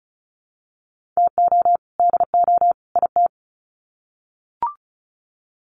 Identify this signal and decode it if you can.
Morse code, heard as a keyed tone, about 34 wpm, 700 Hz. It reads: TO DO IT
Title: Sets of 3 Words 35wpm